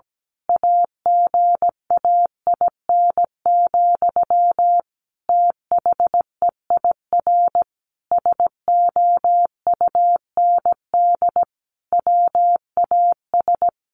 Morse code, heard as a keyed tone, about 17 wpm, 700 Hz. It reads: AGAIN, THEIR SOUND WAS
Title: A Sound Of Thunder 17wpm